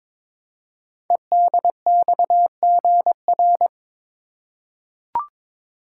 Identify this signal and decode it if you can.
Morse code, heard as a keyed tone, about 22 wpm, 700 Hz. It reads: EDXGR
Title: Code Groups - 5 Letters - 22wpm